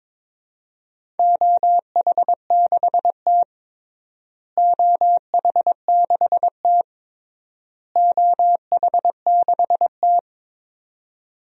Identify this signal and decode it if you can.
Morse code, heard as a keyed tone, about 22 wpm, 700 Hz. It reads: OH6T OH6T OH6T
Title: DXCC-100 Call Signs - 4 Characters - Repeated 3x - 22wpm